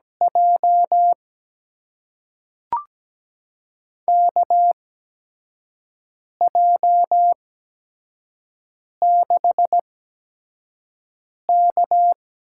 Morse code, heard as a keyed tone, about 17 wpm, spaced as for 5 wpm, 700 Hz. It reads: J KJ6K
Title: USA Call Signs - 6 Characters - 8x Character Spacing - 17wpm